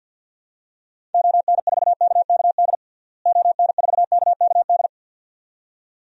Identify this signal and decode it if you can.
Morse code, heard as a keyed tone, about 50 wpm, 700 Hz. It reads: ON4KKD ON4KKD
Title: DXCC-100 Call Signs - 6 Characters - Repeated 2x - 50wpm